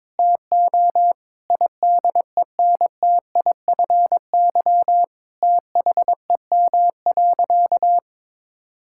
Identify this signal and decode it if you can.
Morse code, heard as a keyed tone, about 22 wpm, 700 Hz. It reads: TO IDENTIFY THEM.
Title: Sentences from Top 1000 Words - Review Part 6 - 22wpm